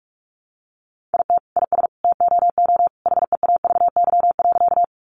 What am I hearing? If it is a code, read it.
Morse code, heard as a keyed tone, about 44 wpm, 700 Hz. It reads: IT IS TOO HEAVY.
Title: Sentences from Top 1100 Words 45wpm